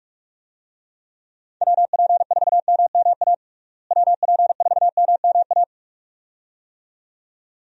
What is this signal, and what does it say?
Morse code, heard as a keyed tone, about 46 wpm, 700 Hz. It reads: WP4MMA WP4MMA
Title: USA Call Signs - 6 Characters - Repeated 2x - 45wpm